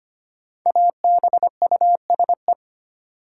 Morse code, heard as a keyed tone, about 25 wpm, 700 Hz. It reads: ABUSE
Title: Sentences from Top 1300 Words - New Word Drill - 25wpm